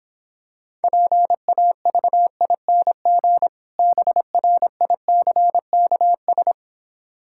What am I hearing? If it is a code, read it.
Morse code, heard as a keyed tone, about 26 wpm, 700 Hz. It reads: PAVING BRICKS